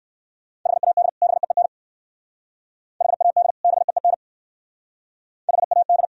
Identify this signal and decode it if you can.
Morse code, heard as a keyed tone, about 60 wpm, 700 Hz. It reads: HAD BEEN HAD BEEN HAD